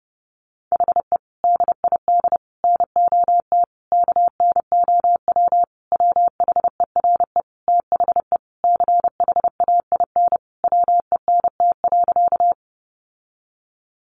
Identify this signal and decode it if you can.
Morse code, heard as a keyed tone, about 30 wpm, 700 Hz. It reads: HE DID NOT KNOW WHERE THE CHAIN WENT.
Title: Sentences from Top 1500 Words 30wpm